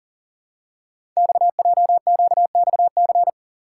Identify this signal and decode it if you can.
Morse code, heard as a keyed tone, about 43 wpm, 690 Hz. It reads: XJQXC
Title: Code Groups - 5 Letters - 40wpm